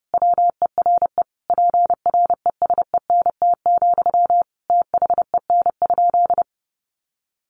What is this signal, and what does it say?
Morse code, heard as a keyed tone, about 30 wpm, 700 Hz. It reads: WERE PRESENT, THEN?
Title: Sentences from Top 800 Words 30wpm